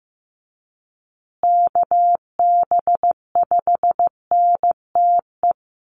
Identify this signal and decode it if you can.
Morse code, heard as a keyed tone, about 15 wpm, 700 Hz. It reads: KB5NTE
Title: USA Call Signs Repeated 3x 15wpm